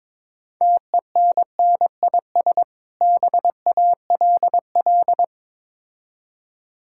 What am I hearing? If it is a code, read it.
Morse code, heard as a keyed tone, about 22 wpm, 690 Hz. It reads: TENNIS BALL
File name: Easy Words - Speed-Racing with Hints - 22wpm